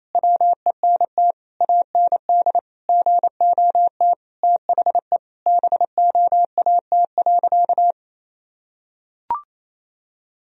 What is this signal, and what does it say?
Morse code, heard as a keyed tone, about 28 wpm, 700 Hz. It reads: WENT AND GOT THE BOAT.
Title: Sentences from Top 1000 Words - Review Part 3 - 28wpm